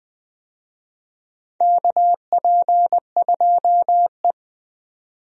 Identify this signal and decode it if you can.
Morse code, heard as a keyed tone, about 20 wpm, 700 Hz. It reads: KP2E
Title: DXCC-100 Call Signs - 4 Characters - Repeated 3x - 20wpm